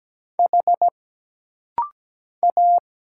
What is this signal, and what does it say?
Morse code, heard as a keyed tone, about 17 wpm, 700 Hz. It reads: H A
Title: Single Letter-Number Alt - Rapid-Fire 17wpm